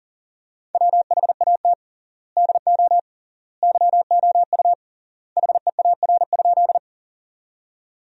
Morse code, heard as a keyed tone, about 40 wpm, 700 Hz. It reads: WHAT DO YOU HEAR?
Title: 3x Word-Spacing - Sentences from Top 300 Words - 40wpm